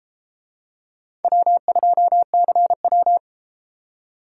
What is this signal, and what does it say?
Morse code, heard as a keyed tone, about 33 wpm, 700 Hz. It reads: W2CW